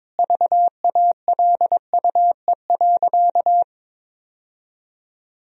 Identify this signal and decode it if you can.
Morse code, heard as a keyed tone, about 22 wpm, 700 Hz. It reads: VALUE.
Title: Sentences from Top 1000 Words - Review Part 3 - 22wpm